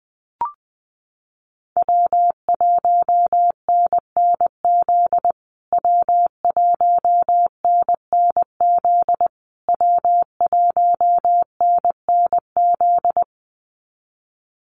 Morse code, heard as a keyed tone, about 20 wpm, 700 Hz. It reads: W1NNZ W1NNZ W1NNZ
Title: USA Call Signs Repeated 3x 20wpm